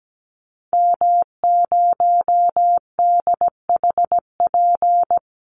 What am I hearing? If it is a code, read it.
Morse code, heard as a keyed tone, about 17 wpm, 700 Hz. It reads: M0DHP
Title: Call Signs - 17wpm